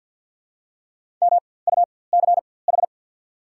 Morse code, heard as a keyed tone, about 54 wpm, spaced as for 24 wpm, 690 Hz. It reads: MUCH
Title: 4x Character Spacing - Top 500 Words - 50wpm